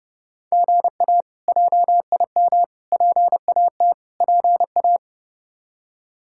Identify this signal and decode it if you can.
Morse code, heard as a keyed tone, about 30 wpm, 700 Hz. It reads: GA JIM PAT PA
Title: K1USN SST QSO Practice - 30wpm